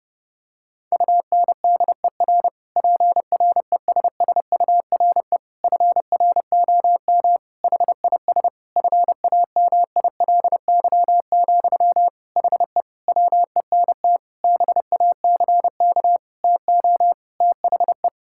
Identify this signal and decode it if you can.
Morse code, heard as a keyed tone, about 32 wpm, 700 Hz. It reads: UNDER PRESSURE FROM HIS FAMILY, HE WENT BACK TO THE
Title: Sentences from Top 1000 Words - Review Part 6 - 30wpm